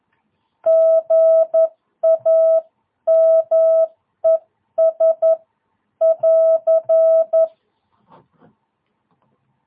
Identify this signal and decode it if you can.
Morse code, heard as a keyed tone, about 11 wpm, spaced as for 9 wpm, 650 Hz. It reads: GAMES<AR>
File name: jh8cbh / 2022cw